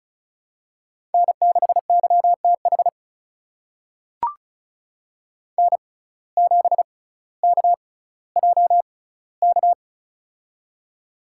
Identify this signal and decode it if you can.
Morse code, heard as a keyed tone, about 34 wpm, spaced as for 18 wpm, 700 Hz. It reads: N6YTH N7KJK
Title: USA Call Signs - 5 Characters - 6x Character Spacing - 35wpm